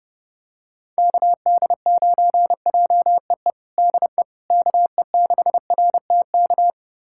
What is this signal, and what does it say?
Morse code, heard as a keyed tone, about 32 wpm, 700 Hz. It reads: KD9JEE DE KE6RTK
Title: USA Callsign DE Callsign - 6 Characters - 30wpm